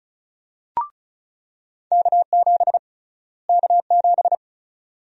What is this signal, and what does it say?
Morse code, heard as a keyed tone, about 36 wpm, 700 Hz. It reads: K7 K7
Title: USA Call Sign Prefixes - Repeated 2x - 35wpm